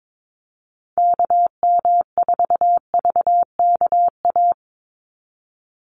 Morse code, heard as a keyed tone, about 22 wpm, 700 Hz. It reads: KM4VKA
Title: USA Call Signs 22wpm